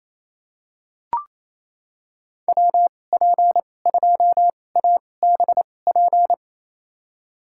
Morse code, heard as a keyed tone, about 28 wpm, spaced as for 20 wpm, 700 Hz. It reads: WP2ABP
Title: USA Call Signs - 6 Characters - 2x Character Spacing - 28wpm